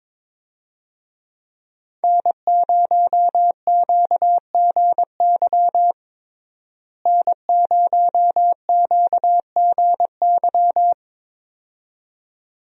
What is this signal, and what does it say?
Morse code, heard as a keyed tone, about 22 wpm, 700 Hz. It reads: N0QGY N0QGY
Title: USA Call Signs - 5 Characters - Repeated 2x - 22wpm